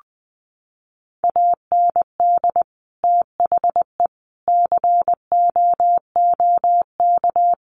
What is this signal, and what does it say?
Morse code, heard as a keyed tone, about 20 wpm, 700 Hz. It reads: AND THE COOK
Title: Sentences from Top 1300 Words 20wpm